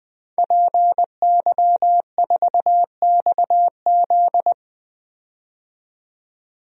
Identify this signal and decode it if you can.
Morse code, heard as a keyed tone, about 20 wpm, 700 Hz. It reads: PY4XZ
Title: DXCC-100 Call Signs - 5 Characters - 20wpm